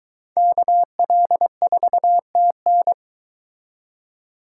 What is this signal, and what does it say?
Morse code, heard as a keyed tone, about 22 wpm, 700 Hz. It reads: KL4TN